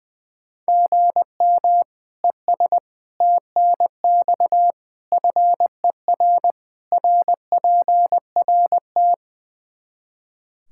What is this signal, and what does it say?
Morse code, heard as a keyed tone, about 20 wpm, 700 Hz. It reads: GM ES TNX FER RPRT
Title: K1USN SST QSO Practice 20wpm